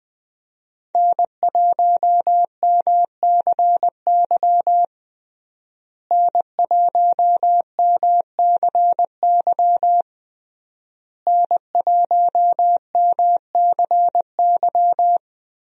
Morse code, heard as a keyed tone, about 20 wpm, 700 Hz. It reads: N1MCY N1MCY N1MCY
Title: USA Call Signs - 5 Characters - Repeated 3x - 20wpm